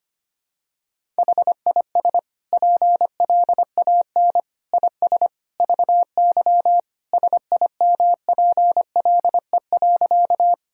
Morse code, heard as a keyed tone, about 25 wpm, 700 Hz. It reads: HIS PLAN IS VY SIMPLE.
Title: Sentences from Top 500 Words and Abbreviations 25wpm